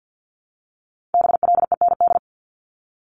Morse code, heard as a keyed tone, about 50 wpm, 700 Hz. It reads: BLEND
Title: Multi-letter Phonemes Repeated 3x 50wpm